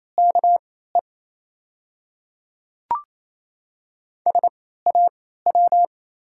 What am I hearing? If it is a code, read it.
Morse code, heard as a keyed tone, about 28 wpm, spaced as for 16 wpm, 700 Hz. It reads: KE SAW
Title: 3x Character Spacing - Top 300 Words - 28wpm